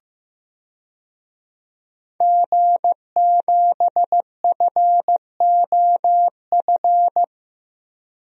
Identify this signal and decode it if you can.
Morse code, heard as a keyed tone, about 15 wpm, 700 Hz. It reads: G7FOF
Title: DXCC-100 Call Signs - 5 Characters - 15wpm